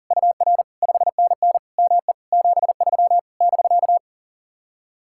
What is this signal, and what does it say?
Morse code, heard as a keyed tone, about 52 wpm, 700 Hz. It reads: UR 5NN ME 73 <BK>